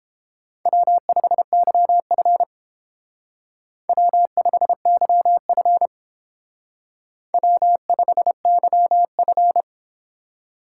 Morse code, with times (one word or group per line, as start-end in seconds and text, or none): 0.7-2.4 W5YF
3.9-5.9 W5YF
7.3-9.6 W5YF